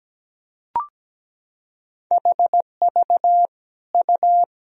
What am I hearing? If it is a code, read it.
Morse code, heard as a keyed tone, about 17 wpm, 700 Hz. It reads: HV U
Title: Sentences from Top 300 Words and Abbreviations 17wpm